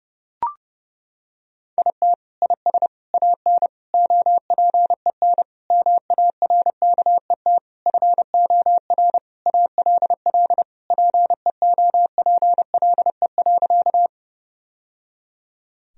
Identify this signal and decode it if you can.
Morse code, heard as a keyed tone, about 32 wpm, 700 Hz. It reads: IT IS AN OPEN MARKET FOR ALL PEOPLE.
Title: Sentences from Top 1000 Words - Review Part 3 - 30wpm